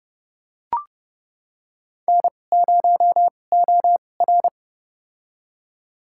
Morse code, heard as a keyed tone, about 30 wpm, spaced as for 22 wpm, 700 Hz. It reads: N0OR
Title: USA Call Signs - 4 Characters - 2x Character Spacing - 30wpm